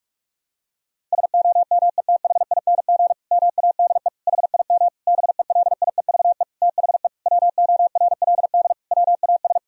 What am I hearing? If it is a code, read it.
Morse code, heard as a keyed tone, about 46 wpm, 700 Hz. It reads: SOMETHING MADE HIM BELIEVE THE WORLD WAS
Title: Sentences from Top 1000 Words - Review Part 1 - 45wpm